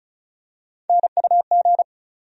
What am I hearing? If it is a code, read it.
Morse code, heard as a keyed tone, about 34 wpm, 700 Hz. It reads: NUG
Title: USA Call Sign Postfixes - 3 Letters 35wpm